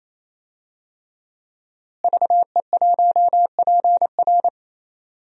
Morse code, heard as a keyed tone, about 28 wpm, 700 Hz. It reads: VE1PR
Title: DXCC-100 Call Signs - 5 Characters - Repeated 3x - 28wpm